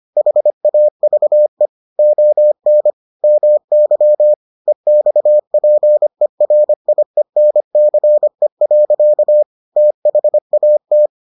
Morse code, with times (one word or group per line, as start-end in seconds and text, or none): 0.2-1.7 HAVE
2.0-2.9 ON
3.2-4.3 MY
4.7-9.4 EXPERIENCE.
9.8-11.1 THAT